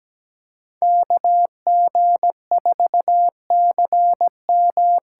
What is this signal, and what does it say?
Morse code, heard as a keyed tone, about 17 wpm, 700 Hz. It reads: KG4CM
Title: USA Call Signs - 6 Characters - 8x Character Spacing - 17wpm